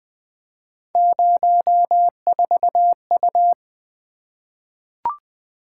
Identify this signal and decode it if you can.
Morse code, heard as a keyed tone, about 20 wpm, 700 Hz. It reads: 04U
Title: Three Letter-Number Combinations - 20wpm